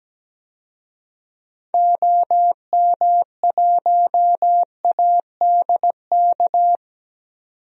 Morse code, heard as a keyed tone, about 17 wpm, 700 Hz. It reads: OM1ADK
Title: DXCC-100 Call Signs - 6 Characters - 17wpm